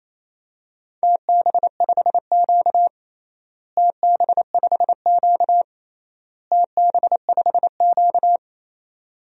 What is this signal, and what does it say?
Morse code, heard as a keyed tone, about 28 wpm, 700 Hz. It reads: TB5Q TB5Q TB5Q
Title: DXCC-100 Call Signs - 4 Characters - Repeated 3x - 28wpm